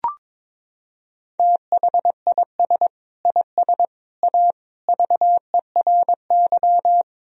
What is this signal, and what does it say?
Morse code, heard as a keyed tone, about 22 wpm, 700 Hz. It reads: THIS IS A VERY
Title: Sentences from Top 1000 Words - Review Part 1 - 22wpm